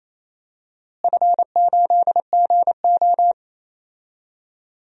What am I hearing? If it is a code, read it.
Morse code, heard as a keyed tone, about 28 wpm, 700 Hz. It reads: F8GO
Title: DXCC-100 Call Signs - 4 Characters - Repeated 3x - 28wpm